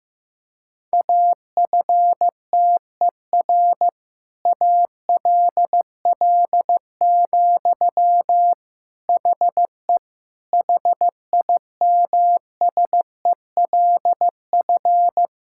Morse code, heard as a keyed tone, about 15 wpm, 700 Hz. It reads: AFTER ALL, HE HIMSELF
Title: Sentences from Top 1600 Words 15wpm